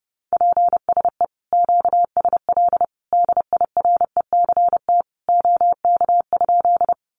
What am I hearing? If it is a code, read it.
Morse code, heard as a keyed tone, about 30 wpm, 700 Hz. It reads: PSE QSL DIRECT OK?
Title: Common QSO Phrases 30wpm